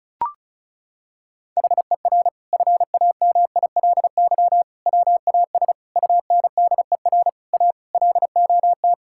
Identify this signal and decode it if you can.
Morse code, heard as a keyed tone, about 36 wpm, 700 Hz. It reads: HER FAMILY WAS UNDER A LOT
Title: Sentences from Top 1000 Words - Review Part 6 - 35wpm